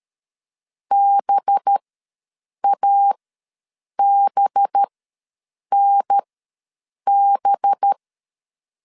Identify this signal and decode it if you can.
Morse code, heard as a keyed tone, about 13 wpm, spaced as for 7 wpm, 800 Hz. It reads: BABNB
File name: LEZIONI / 5 LBVC / 5 LB ws 12wpm 3034